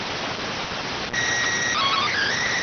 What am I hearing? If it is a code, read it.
SSTV (slow-scan television), PD 290.